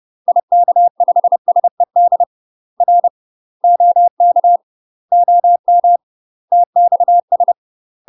Morse code, heard as a keyed tone, about 30 wpm, 700 Hz. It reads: IK5SED R OK OM TXS
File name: UK MASTER QSOs TRAINING / 30WPM / UK MASTER QSOs 0036